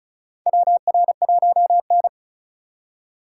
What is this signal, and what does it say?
Morse code, heard as a keyed tone, about 34 wpm, 700 Hz. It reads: WR1N
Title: USA Call Signs - 4 Characters - 2x Character Spacing - 35wpm